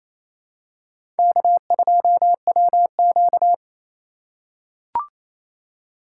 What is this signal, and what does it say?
Morse code, heard as a keyed tone, about 28 wpm, 700 Hz. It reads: K2WQ
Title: USA Call Signs - 4 Characters - 8x Character Spacing - 28wpm